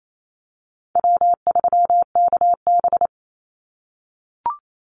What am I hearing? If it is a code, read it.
Morse code, heard as a keyed tone, about 28 wpm, 700 Hz. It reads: W3KB
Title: Easy Call Signs - 28wpm